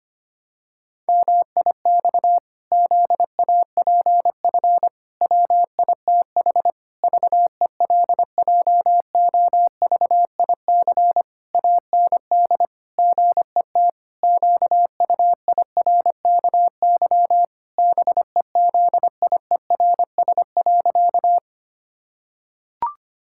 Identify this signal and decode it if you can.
Morse code, heard as a keyed tone, about 25 wpm, 700 Hz. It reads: MIX ZAPF WITH VELJOVIC AND GET QUIRKY BEZIERS.
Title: Pangram Challenge v2 - 25wpm